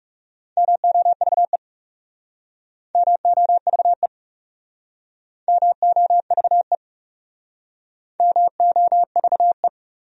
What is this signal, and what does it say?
Morse code, heard as a keyed tone, about 39 wpm, 700 Hz. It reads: MOVE MOVE MOVE MOVE